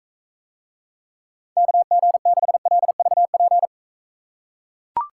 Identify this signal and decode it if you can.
Morse code, heard as a keyed tone, about 43 wpm, 700 Hz. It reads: KG6LVP